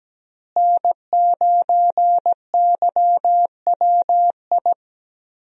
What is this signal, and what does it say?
Morse code, heard as a keyed tone, about 17 wpm, 700 Hz. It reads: N9YWI
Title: USA Call Signs - 5 Characters - Repeated 3x - 17wpm